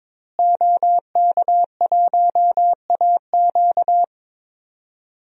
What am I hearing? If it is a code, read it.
Morse code, heard as a keyed tone, about 22 wpm, 700 Hz. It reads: OK1AQ
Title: DXCC-100 Call Signs - 5 Characters - Repeated 2x - 22wpm